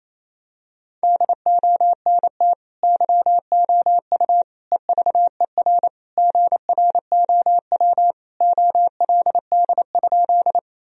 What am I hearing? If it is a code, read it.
Morse code, heard as a keyed tone, about 28 wpm, 700 Hz. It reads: DONT YOU EVER GROW OLD?
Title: Sentences from Top 1000 Words - Review Part 1 - 28wpm